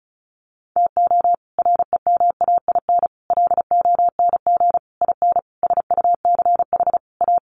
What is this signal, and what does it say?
Morse code, heard as a keyed tone, about 34 wpm, 700 Hz. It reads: TO REMAIN LONG IN SUCH A
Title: Sentences from Top 1500 Words 35wpm